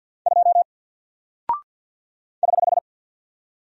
Morse code, heard as a keyed tone, about 49 wpm, 700 Hz. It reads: 2 <HH>
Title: Single Character (Letter:Number:Punctuation:Prosign) Alt 50wpm